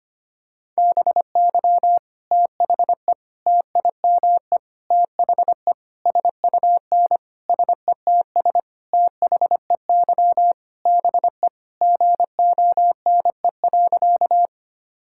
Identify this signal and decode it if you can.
Morse code, heard as a keyed tone, about 25 wpm, 700 Hz. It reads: BY THE TIME THE SUN SETS THEY BE GONE.
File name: practice / 25wpm / Sentences with 4-Word Phrases - 25wpm